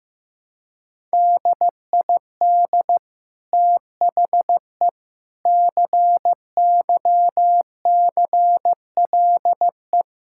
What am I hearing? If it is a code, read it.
Morse code, heard as a keyed tone, about 15 wpm, 700 Hz. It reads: DID THE CYCLE
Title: Sentences from Top 1000 Words - Review Part 6 - 15wpm